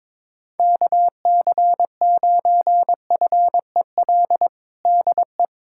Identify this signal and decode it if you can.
Morse code, heard as a keyed tone, about 22 wpm, 700 Hz. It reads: KC9FEL DE
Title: USA Callsign DE Callsign - 6 Characters - 22wpm